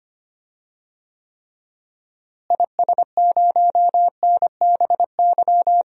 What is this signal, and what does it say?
Morse code, heard as a keyed tone, about 25 wpm, 700 Hz. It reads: IS0NBY
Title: DXCC-100 Call Signs - 6 Characters - Repeated 3x - 25wpm